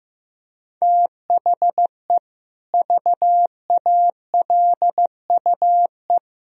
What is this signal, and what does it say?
Morse code, heard as a keyed tone, about 15 wpm, 700 Hz. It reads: THE VALUE
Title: Sentences from Top 1000 Words - Review Part 3 - 15wpm